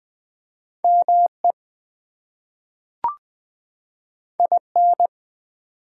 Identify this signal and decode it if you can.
Morse code, heard as a keyed tone, about 20 wpm, 700 Hz. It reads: ME IN
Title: Two Letter Words - 20wpm